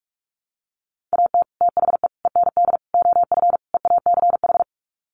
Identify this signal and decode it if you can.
Morse code, heard as a keyed tone, about 44 wpm, 700 Hz. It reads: AT THE END OF EACH
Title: Sets of 5 Words Encore 45wpm